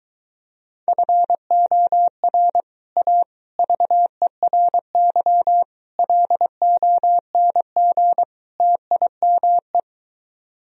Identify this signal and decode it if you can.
Morse code, heard as a keyed tone, about 22 wpm, 700 Hz. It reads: FOR A VERY LONG TIME